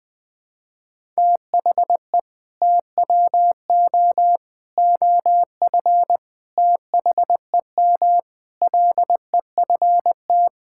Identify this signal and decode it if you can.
Morse code, heard as a keyed tone, about 20 wpm, 700 Hz. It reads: THE TWO OF THEM LEFT
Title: Sentences with 4-Word Phrases - 20wpm